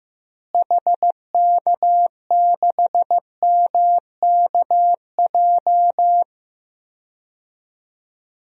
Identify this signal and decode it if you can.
Morse code, heard as a keyed tone, about 16 wpm, 700 Hz. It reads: HK6MKJ
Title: DXCC-100 Call Signs - 6 Characters - Repeated 3x - 15wpm